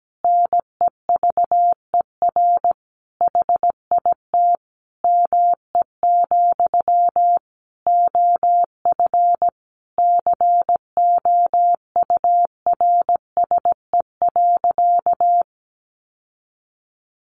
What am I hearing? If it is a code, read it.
Morse code, heard as a keyed tone, about 17 wpm, 700 Hz. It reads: NEVER HIT ME, OF COURSE.
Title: Sentences from Top 1200 Words 17wpm